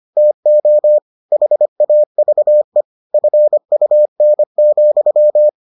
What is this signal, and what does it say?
Morse code, heard as a keyed tone, about 25 wpm, 600 Hz. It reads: TO HAVE FUN,